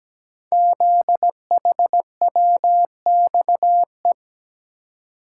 Morse code, heard as a keyed tone, about 17 wpm, 700 Hz. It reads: ZHWXE
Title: Code Groups - 5 Letters - 17wpm